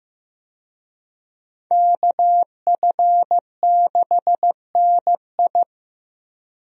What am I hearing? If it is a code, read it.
Morse code, heard as a keyed tone, about 15 wpm, 700 Hz. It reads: KF6NI
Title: USA Call Signs - 5 Characters - Repeated 3x - 15wpm